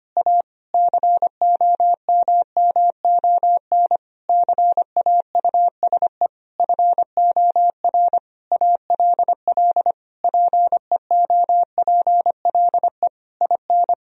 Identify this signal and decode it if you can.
Morse code, heard as a keyed tone, about 25 wpm, 700 Hz. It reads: A COMMON CAUSE FOR ALL PEOPLE IN
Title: Sentences from Top 1000 Words - Review Part 3 - 25wpm